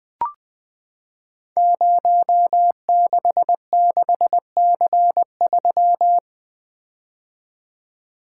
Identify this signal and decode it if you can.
Morse code, heard as a keyed tone, about 20 wpm, 700 Hz. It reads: 066C3
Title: Code Groups - 5 Letter:Numbers - 20wpm